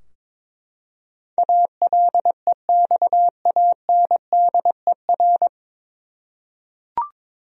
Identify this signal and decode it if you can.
Morse code, heard as a keyed tone, about 21 wpm, 700 Hz. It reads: ALEXANDER
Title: USA Common Names - 22wpm